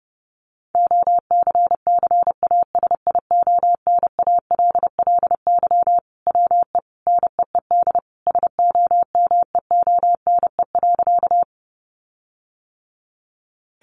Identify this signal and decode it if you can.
Morse code, heard as a keyed tone, about 28 wpm, 700 Hz. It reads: OCCASIONALLY WE NEED SOMEONE.
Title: Sentences from Top 1300 Words 30wpm